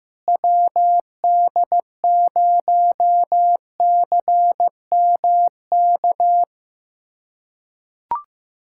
Morse code, heard as a keyed tone, about 15 wpm, 700 Hz. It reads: WD0CMK
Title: USA Callsign DE Callsign - 6 Characters - 15wpm